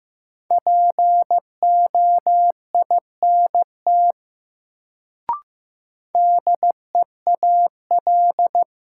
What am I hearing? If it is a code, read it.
Morse code, heard as a keyed tone, about 15 wpm, 700 Hz. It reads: POINT DEAL
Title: Sets of 3 Words - New Word Drill - 15wpm